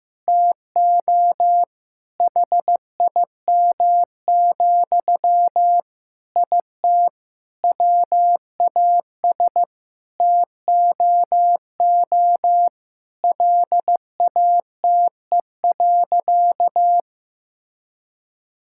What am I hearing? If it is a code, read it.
Morse code, heard as a keyed tone, about 15 wpm, 700 Hz. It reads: TO HIM, IT WAS TOO LATE.
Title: Sentences from Top 1000 Words - Review Part 6 - 15wpm